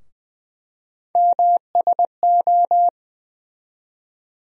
Morse code, heard as a keyed tone, about 20 wpm, 700 Hz. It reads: MSO
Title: USA Call Sign Postfixes - 3 Letters 20wpm